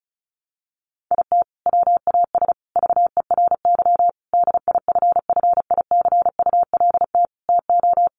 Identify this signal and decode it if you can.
Morse code, heard as a keyed tone, about 34 wpm, 700 Hz. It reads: IT WAS VERY DIFFICULT TO
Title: Sentences from Top 1100 Words 35wpm